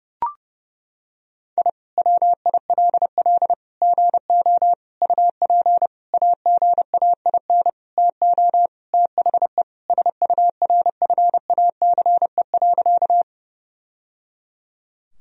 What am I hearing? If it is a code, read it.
Morse code, heard as a keyed tone, about 32 wpm, 700 Hz. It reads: I WILL GO UP AGAIN TO THE SURFACE.
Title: Sentences from Top 1000 Words - Review Part 3 - 30wpm